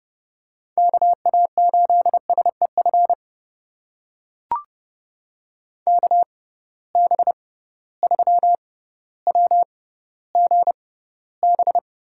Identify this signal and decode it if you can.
Morse code, heard as a keyed tone, about 30 wpm, spaced as for 15 wpm, 700 Hz. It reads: KA8SEF KB3WGB
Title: USA Call Signs - 6 Characters - 6x Character Spacing - 30wpm